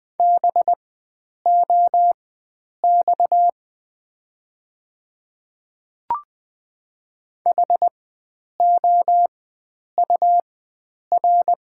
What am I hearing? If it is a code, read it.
Morse code, heard as a keyed tone, about 20 wpm, spaced as for 9 wpm, 700 Hz. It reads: BOX HOUR
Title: 4x Character Spacing - Top 500 Words - 20wpm